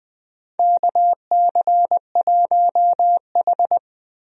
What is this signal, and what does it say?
Morse code, heard as a keyed tone, about 20 wpm, 700 Hz. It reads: KC1H
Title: USA Call Signs - 4 Characters - 20wpm